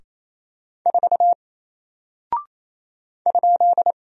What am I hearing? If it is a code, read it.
Morse code, heard as a keyed tone, about 27 wpm, 700 Hz. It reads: 4 ?
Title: Single Character (Letter:Number:Punctuation:Prosign) 28wpm